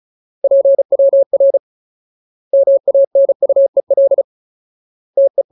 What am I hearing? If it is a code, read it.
Morse code, heard as a keyed tone, about 34 wpm, 550 Hz. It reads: PWR MANUEL TE
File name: Ultimate Word List / 35WPM Ultimate